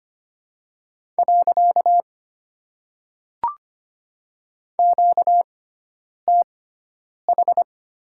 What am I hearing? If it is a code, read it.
Morse code, heard as a keyed tone, about 25 wpm, spaced as for 9 wpm, 700 Hz. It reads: . QTH
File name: Top 90 QSO Elements - 6x Character Spacing - 25wpm